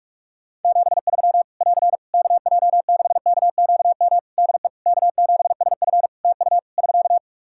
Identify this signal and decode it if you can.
Morse code, heard as a keyed tone, about 46 wpm, 700 Hz. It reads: 73 <AR> KJ6KQM DE K7SF TU <SK>
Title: Practice QSO - Exchange 4 - Ending - 45wpm